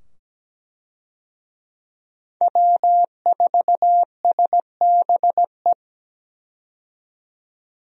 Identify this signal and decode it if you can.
Morse code, heard as a keyed tone, about 17 wpm, 700 Hz. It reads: W4SBE
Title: USA Call Signs - 5 Characters - 17wpm